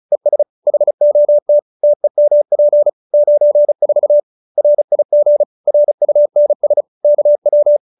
Morse code, heard as a keyed tone, about 34 wpm, 590 Hz. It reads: ES HOT TEMP 94 RIG RUNS KW